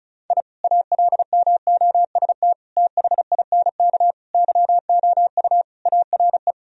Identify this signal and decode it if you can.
Morse code, heard as a keyed tone, about 36 wpm, 700 Hz. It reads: I ALMOST THINK YOU ARE
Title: Sentences from Top 1000 Words - Review Part 4 - 35wpm